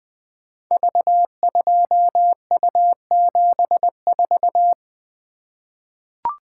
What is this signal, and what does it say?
Morse code, heard as a keyed tone, about 20 wpm, 700 Hz. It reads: V2U74
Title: Code Groups - 5 Letter:Numbers - 20wpm